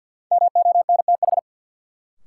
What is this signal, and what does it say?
Morse code, heard as a keyed tone, about 51 wpm, 700 Hz. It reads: MONTH
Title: Sentences from Top 800 Words - New Word Drill - 50wpm